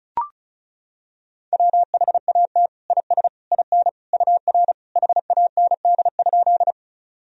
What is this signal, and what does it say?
Morse code, heard as a keyed tone, about 34 wpm, 700 Hz. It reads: WHAT IS IN UR HAND?
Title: Sentences from Top 200 Words and Abbreviations 35wpm